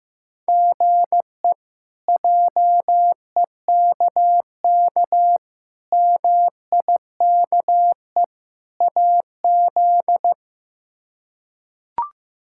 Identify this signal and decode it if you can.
Morse code, heard as a keyed tone, about 16 wpm, 700 Hz. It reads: GE JEKK MIKE AZ
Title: K1USN SST QSO Practice - 15wpm